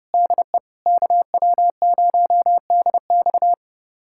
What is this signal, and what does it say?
Morse code, heard as a keyed tone, about 32 wpm, 700 Hz. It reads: DE KW0DX
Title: USA Callsign DE Callsign - 5 Characters - 30wpm